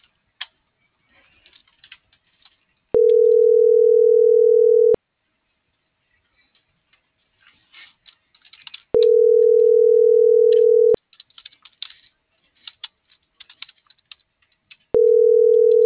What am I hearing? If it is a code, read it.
Ringback tone (North America), 440 + 480 Hz.